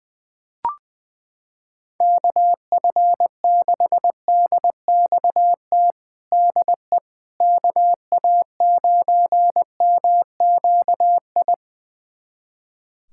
Morse code, heard as a keyed tone, about 21 wpm, 700 Hz. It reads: KF6DXT DE KA9MQI
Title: USA Callsign DE Callsign - 6 Characters - 20wpm